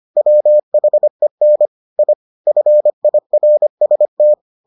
Morse code, heard as a keyed tone, about 25 wpm, 600 Hz. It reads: WHEN I FIRST